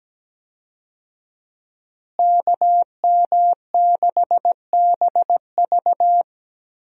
Morse code, heard as a keyed tone, about 17 wpm, 700 Hz. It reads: KM6BV
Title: USA Call Signs - 5 Characters - Repeated 2x - 17wpm